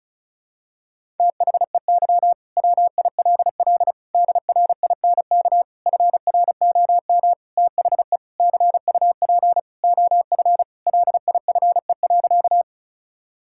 Morse code, heard as a keyed tone, about 36 wpm, 700 Hz. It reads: THEY WILL DRINK FROM THE CUP OF LIFE.
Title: Sentences from Top 1600 Words 35wpm